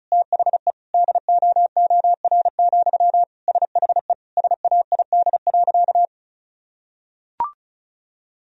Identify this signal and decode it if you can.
Morse code, heard as a keyed tone, about 36 wpm, 700 Hz. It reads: THE DOOR, SHE SAID.
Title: Sentences from Top 1000 Words - Review Part 2 - 35wpm